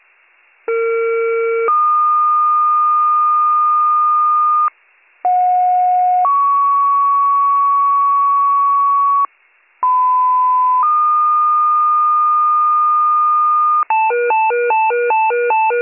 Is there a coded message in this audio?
Yes: two-tone page from 0.7 to 13.8 s, heard as a one-second tone, then a three-second tone, about 460 then 1220 Hz, about 730 then 1120 Hz, about 1010 then 1220 Hz.